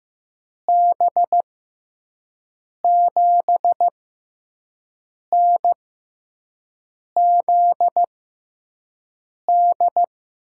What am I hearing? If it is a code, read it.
Morse code, heard as a keyed tone, about 15 wpm, spaced as for 5 wpm, 700 Hz. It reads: B7NZD
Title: USA Call Signs - 6 Characters - 6x Character Spacing - 15wpm